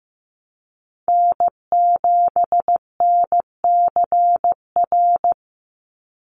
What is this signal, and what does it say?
Morse code, heard as a keyed tone, about 15 wpm, 700 Hz. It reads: N7NCR
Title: USA Call Signs 15wpm